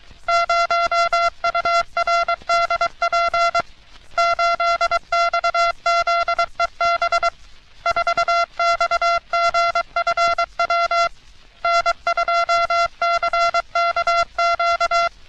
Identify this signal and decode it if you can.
Morse code, heard as a keyed tone, about 23 wpm, 680 Hz. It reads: 0URDP 8XZEB 4XGFW N2CKQ
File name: U.S. Navy - RN-2849-124 - Sixteens Morse Code Practice